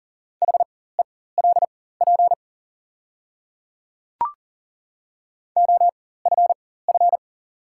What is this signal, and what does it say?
Morse code, heard as a keyed tone, about 38 wpm, spaced as for 18 wpm, 700 Hz. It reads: HELP OFF
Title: 4x Character Spacing - Top 200 Words - 40wpm